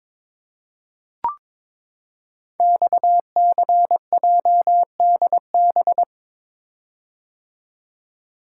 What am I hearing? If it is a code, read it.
Morse code, heard as a keyed tone, about 23 wpm, 700 Hz. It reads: XCJDB
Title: Code Groups - 5 Letters - 22wpm